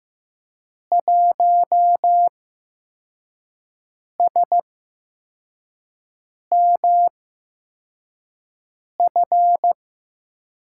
Morse code, heard as a keyed tone, about 15 wpm, spaced as for 4 wpm, 700 Hz. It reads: 1SMF
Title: USA Call Signs - 5 Characters - 8x Character Spacing - 15wpm